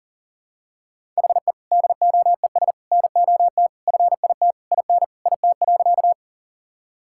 Morse code, heard as a keyed tone, about 40 wpm, 700 Hz. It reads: HE DOES NOT FIT IN IT.
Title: Sentences from Top 1000 Words - Review Part 5 - 40wpm